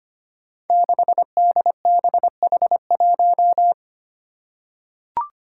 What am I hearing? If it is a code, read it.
Morse code, heard as a keyed tone, about 25 wpm, 700 Hz. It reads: 6DBH1
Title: Five Challenging Letter-Number Combinations 25wpm